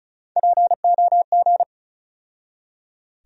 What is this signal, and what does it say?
Morse code, heard as a keyed tone, about 35 wpm, 700 Hz. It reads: POG
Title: USA Call Sign Postfixes - 3 Letters 35wpm